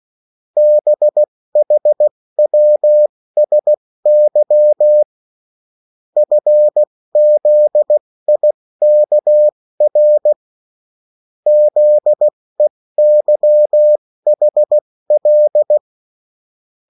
Morse code, heard as a keyed tone, about 16 wpm, spaced as for 14 wpm, 600 Hz. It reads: BHWSY FZIKR ZEYHL
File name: morse mp3 / revision9-vitesse 14 mots